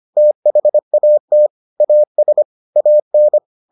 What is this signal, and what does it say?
Morse code, heard as a keyed tone, about 25 wpm, 600 Hz. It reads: THAT AS AN